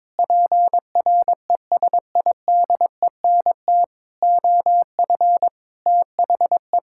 Morse code, heard as a keyed tone, about 22 wpm, 700 Hz. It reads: PRESIDENT OF THE
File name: Sets of 9 Words - 22wpm